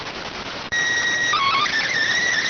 SSTV (slow-scan television), PD 290.